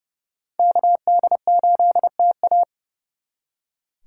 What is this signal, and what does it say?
Morse code, heard as a keyed tone, about 37 wpm, 700 Hz. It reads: KD8TA
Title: Speed Racing - USA Call Signs 30wpm